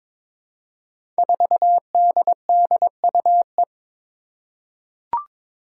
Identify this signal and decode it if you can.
Morse code, heard as a keyed tone, about 22 wpm, 700 Hz. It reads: 4DDUE
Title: Five Challenging Letter-Number Combinations 22wpm